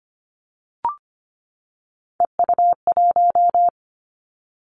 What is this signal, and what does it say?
Morse code, heard as a keyed tone, about 25 wpm, 700 Hz. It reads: EU1
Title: Callsign Prefixes for Top 100 DXCC - 25wpm